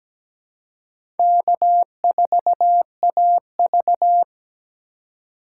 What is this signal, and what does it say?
Morse code, heard as a keyed tone, about 17 wpm, 700 Hz. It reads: K4AV
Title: USA Call Signs - 4 Characters - 3x Character Spacing - 17wpm